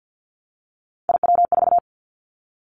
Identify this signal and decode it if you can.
Morse code, heard as a keyed tone, about 50 wpm, 700 Hz. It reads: IW4
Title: Callsign Prefixes for Top 50 DXCC - 50wpm